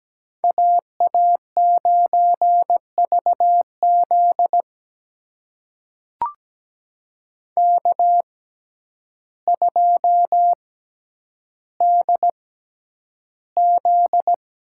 Morse code, heard as a keyed tone, about 17 wpm, spaced as for 9 wpm, 700 Hz. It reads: AA9VZ K2DZ
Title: USA Call Signs - 6x Character Spacing - 17wpm